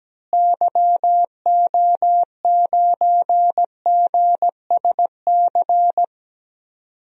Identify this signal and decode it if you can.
Morse code, heard as a keyed tone, about 17 wpm, 700 Hz. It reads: YO9GSC
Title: DXCC-100 Call Signs - 6 Characters - Repeated 3x - 17wpm